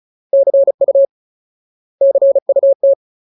Morse code, heard as a keyed tone, about 34 wpm, 550 Hz. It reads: CU CUT
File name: Ultimate Word List / 35WPM Ultimate